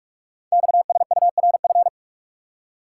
Morse code, heard as a keyed tone, about 44 wpm, 700 Hz. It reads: XSURF
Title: Code Groups - 5 Letters - 45wpm